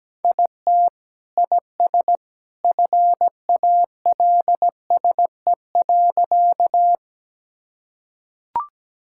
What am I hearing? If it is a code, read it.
Morse code, heard as a keyed tone, about 17 wpm, 700 Hz. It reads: IT IS FALSE.
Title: Sentences from Top 1600 Words 17wpm